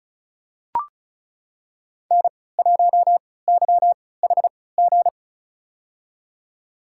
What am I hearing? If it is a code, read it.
Morse code, heard as a keyed tone, about 34 wpm, spaced as for 20 wpm, 700 Hz. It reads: N1YHG
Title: USA Call Signs - 5 Characters - 3x Character Spacing - 35wpm